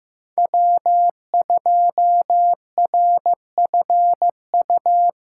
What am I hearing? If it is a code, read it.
Morse code, heard as a keyed tone, about 15 wpm, 700 Hz. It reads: W2RFU
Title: USA Call Signs - 5 Characters - 6x Character Spacing - 15wpm